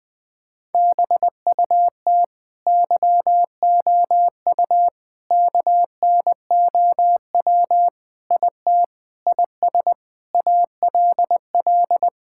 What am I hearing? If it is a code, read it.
Morse code, heard as a keyed tone, about 20 wpm, 700 Hz. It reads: BUT YOU KNOW IT IS ALL
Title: Sentences from Top 1600 Words 20wpm